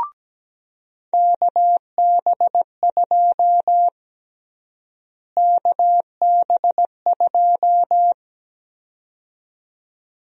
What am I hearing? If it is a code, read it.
Morse code, heard as a keyed tone, about 17 wpm, 700 Hz. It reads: KB2 KB2
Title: USA Call Sign Prefixes - Repeated 2x - 17wpm